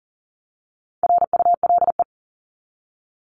Morse code, heard as a keyed tone, about 40 wpm, 700 Hz. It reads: RULE
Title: ES:S Suffix - Top 100 Words with Base Words 40wpm